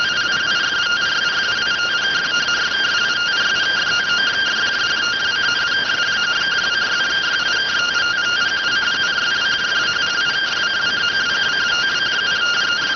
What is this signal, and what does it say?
Two-tone teleprinter signal, about 170 Hz shift.